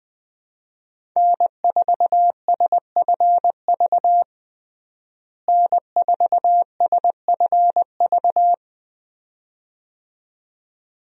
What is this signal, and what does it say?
Morse code, heard as a keyed tone, about 20 wpm, 700 Hz. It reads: N4SFV N4SFV
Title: USA Call Signs - 5 Characters - Repeated 2x - 20wpm